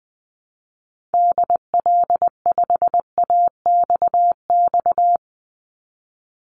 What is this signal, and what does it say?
Morse code, heard as a keyed tone, about 20 wpm, 700 Hz. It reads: DL5AXX
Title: Call Signs - 20wpm